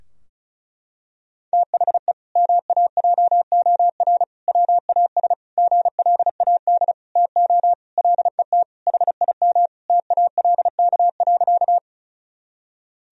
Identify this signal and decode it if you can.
Morse code, heard as a keyed tone, about 36 wpm, 700 Hz. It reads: THE MAJOR WAS GLAD TO LET HIM TALK.
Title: Sentences from Top 1000 Words - Review Part 4 - 35wpm